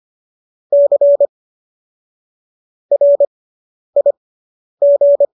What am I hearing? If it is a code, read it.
Morse code, heard as a keyed tone, about 25 wpm, spaced as for 10 wpm, 570 Hz. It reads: C RIG